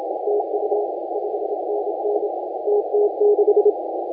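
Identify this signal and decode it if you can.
Morse code, heard as a keyed tone, about 27 wpm, 410 Hz, with radio noise. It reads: TU 5NN TT6
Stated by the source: Morse contest audio recorded by LA4C during the SAC CW 2018 contest.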